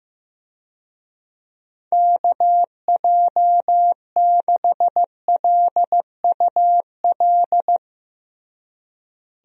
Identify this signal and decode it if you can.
Morse code, heard as a keyed tone, about 16 wpm, 700 Hz. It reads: KJ6LUL
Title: USA Call Signs - 6 Characters - Repeated 2x - 15wpm